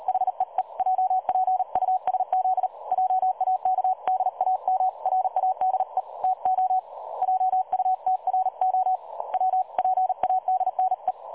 Morse code, heard as a keyed tone, about 38 wpm, 750 Hz. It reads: HEE JPUSZ JAK DAM RADE TO JUTRO WPADNE
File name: dlazaawansowanych / nagraniazpasma / websdr recording 3565.0kHz